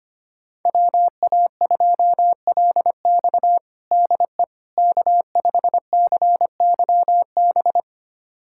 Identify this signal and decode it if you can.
Morse code, heard as a keyed tone, about 25 wpm, 700 Hz. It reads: WA2LX DE K5CYB
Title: USA Callsign DE Callsign - 5 Characters - 25wpm